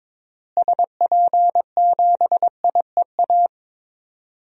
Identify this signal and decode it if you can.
Morse code, heard as a keyed tone, about 22 wpm, 700 Hz. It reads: SP7IEA
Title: DXCC-100 Call Signs - 6 Characters - Repeated 3x - 22wpm